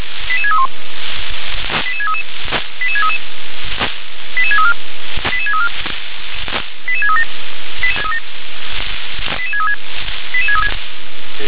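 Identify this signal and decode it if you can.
Five-tone selective call (ZVEI), heard as quick runs of five tones, tones of about 70 ms. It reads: 88631 88633 88633 88634 88634 88637 88636 88636
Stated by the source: ZVEI five-tone selective call received by SDR (357).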